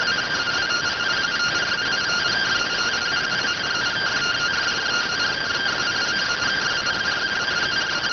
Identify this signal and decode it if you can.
Two-tone teleprinter signal, about 170 Hz shift.